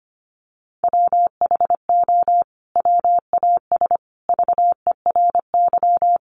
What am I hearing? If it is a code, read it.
Morse code, heard as a keyed tone, about 25 wpm, 700 Hz. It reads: WHO WAS VERY
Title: Sentences from Top 1400 Words 25wpm